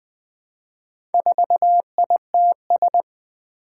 Morse code, heard as a keyed tone, about 20 wpm, 700 Hz. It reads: 4ITS
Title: Four Challenging Letter-Number Combinations 20wpm